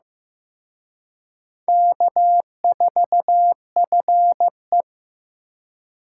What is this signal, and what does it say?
Morse code, heard as a keyed tone, about 18 wpm, 700 Hz. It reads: K4FE
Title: Speed Racing - USA Call Signs 15wpm